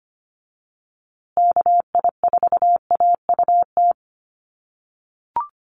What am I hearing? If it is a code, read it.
Morse code, heard as a keyed tone, about 25 wpm, 700 Hz. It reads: KI4AUT
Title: USA Call Signs Repeated 2x 25wpm